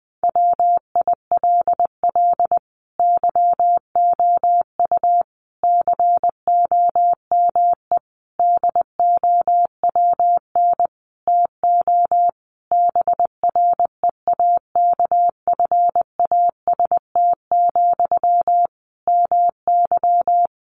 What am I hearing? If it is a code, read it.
Morse code, heard as a keyed tone, about 20 wpm, 700 Hz. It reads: WILL YOU COME DOWN TO BREAKFAST, MY
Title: Sentences from Top 1500 Words 20wpm